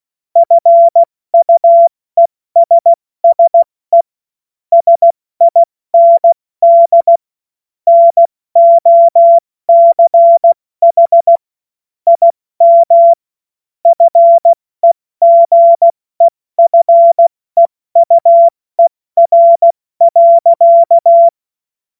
Morse code, heard as a keyed tone, about 16 wpm, 680 Hz. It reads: FUESSE SIND NOCH IM FEGEFEUER.
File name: Morsekurs / Witze / Witz 17.Oktober 2023 70 F80